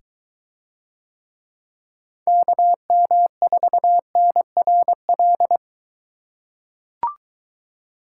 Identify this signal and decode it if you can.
Morse code, heard as a keyed tone, about 22 wpm, 700 Hz. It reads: KM4NRL